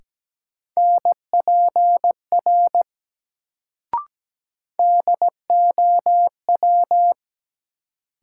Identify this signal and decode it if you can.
Morse code, heard as a keyed tone, about 17 wpm, 700 Hz. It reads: NPR DOW
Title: Sets of 5 Words - New Word Drill - 17wpm